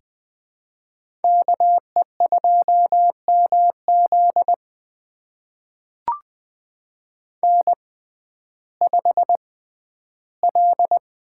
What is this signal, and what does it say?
Morse code, heard as a keyed tone, about 20 wpm, spaced as for 12 wpm, 700 Hz. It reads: KE2MZ N5L
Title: USA Call Signs - 5 Characters - 6x Character Spacing - 20wpm